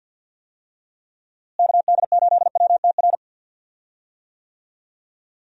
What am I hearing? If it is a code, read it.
Morse code, heard as a keyed tone, about 51 wpm, 700 Hz. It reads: KD8WTR
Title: USA Call Signs - 6 Characters - Repeated 3x - 50wpm